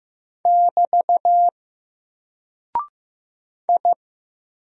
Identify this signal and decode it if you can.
Morse code, heard as a keyed tone, about 15 wpm, 700 Hz. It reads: <BT> I
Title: Single Character (Letter:Number:Punctuation:Prosign) Alt 15wpm